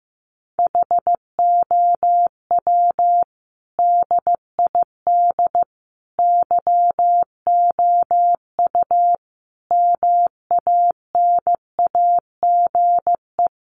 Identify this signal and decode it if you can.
Morse code, heard as a keyed tone, about 15 wpm, 700 Hz. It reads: HOW DID YOU MANAGE
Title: Sentences from Top 1400 Words 15wpm